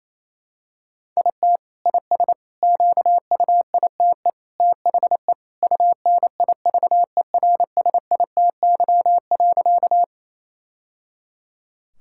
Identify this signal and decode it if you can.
Morse code, heard as a keyed tone, about 28 wpm, 700 Hz. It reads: IT IS QUITE THE UNIVERSITY.
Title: Sentences from Top 1000 Words - Review Part 5 - 28wpm